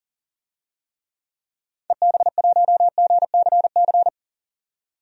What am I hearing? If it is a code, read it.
Morse code, heard as a keyed tone, about 37 wpm, 700 Hz. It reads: EB1GCC
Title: DXCC-100 Call Signs - 6 Characters - 40wpm